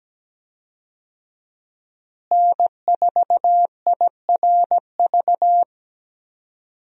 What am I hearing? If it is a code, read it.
Morse code, heard as a keyed tone, about 17 wpm, 700 Hz. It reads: N4IRV
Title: USA Call Signs - 5 Characters - Repeated 2x - 17wpm